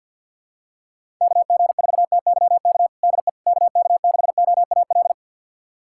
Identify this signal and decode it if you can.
Morse code, heard as a keyed tone, about 47 wpm, 700 Hz. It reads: KG4TYK DE KK6OAL
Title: USA Callsign DE Callsign - 6 Characters - 50wpm